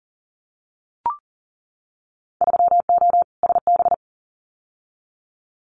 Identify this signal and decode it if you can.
Morse code, heard as a keyed tone, about 41 wpm, 700 Hz. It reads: 3O SB
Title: Winter Field Day Exchanges Encore 40wpm